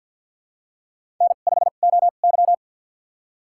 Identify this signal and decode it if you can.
Morse code, heard as a keyed tone, about 51 wpm, spaced as for 36 wpm, 700 Hz. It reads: N5OY
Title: USA Call Signs - 2x Character Spacing - 50wpm